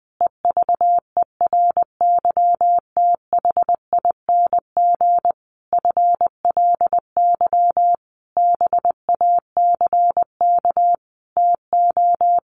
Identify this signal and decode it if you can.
Morse code, heard as a keyed tone, about 20 wpm, 700 Hz. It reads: EVERYTHING FLY BACK TO
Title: A Sound Of Thunder 20wpm